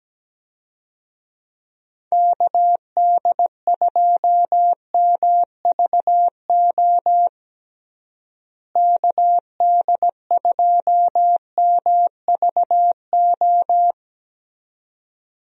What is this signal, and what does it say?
Morse code, heard as a keyed tone, about 17 wpm, 700 Hz. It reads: KD2MVO KD2MVO
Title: USA Call Signs - 6 Characters - Repeated 2x - 17wpm